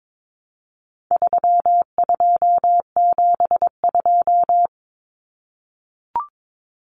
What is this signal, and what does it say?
Morse code, heard as a keyed tone, about 22 wpm, 700 Hz. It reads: 3272
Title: Common Cut Numbers - 4-Digit 22wpm